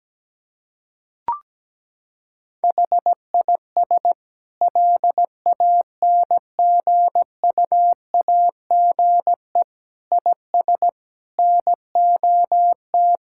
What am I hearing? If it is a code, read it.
Morse code, heard as a keyed tone, about 17 wpm, 700 Hz. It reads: HIS LANGUAGE IS NOT
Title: Sentences from Top 1600 Words 17wpm